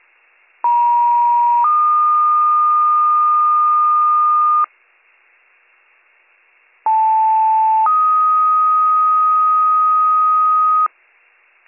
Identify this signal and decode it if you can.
Two-tone page, heard as a one-second tone, then a three-second tone, about 950 then 1220 Hz, about 880 then 1250 Hz. It is heard from 0.6 to 10.9 s.